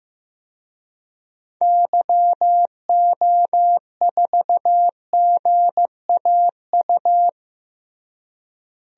Morse code, heard as a keyed tone, about 15 wpm, 700 Hz. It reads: YO4GAU
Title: DXCC-100 Call Signs - 6 Characters - Repeated 3x - 15wpm